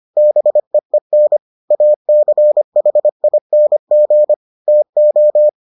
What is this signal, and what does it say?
Morse code, heard as a keyed tone, about 25 wpm, 600 Hz. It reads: BEEN ACHING TO